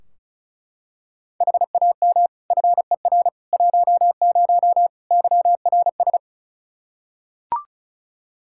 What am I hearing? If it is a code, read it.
Morse code, heard as a keyed tone, about 36 wpm, 700 Hz. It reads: HAM FER 10 YRS
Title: Practice QSO - Exchange 3 - More Info - 35wpm